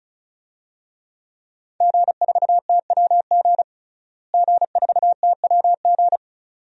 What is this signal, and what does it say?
Morse code, heard as a keyed tone, about 36 wpm, 700 Hz. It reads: G4TWG G4TWG
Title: DXCC-100 Call Signs - 5 Characters - Repeated 2x - 35wpm